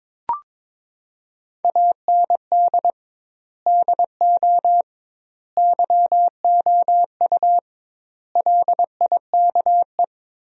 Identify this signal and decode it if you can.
Morse code, heard as a keyed tone, about 21 wpm, 700 Hz. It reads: AND DO YOU LIKE
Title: 2x Word-Spacing - Sentences from Top 100 Words - 22wpm